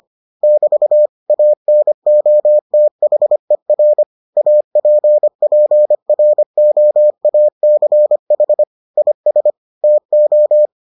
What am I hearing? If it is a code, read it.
Morse code, heard as a keyed tone, about 25 wpm, 600 Hz. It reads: <BT> ANOTHER APPROACH IS TO